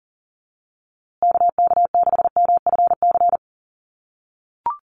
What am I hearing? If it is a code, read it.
Morse code, heard as a keyed tone, about 40 wpm, 700 Hz. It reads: KK6MFC
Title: USA Call Signs Repeated 2x 40wpm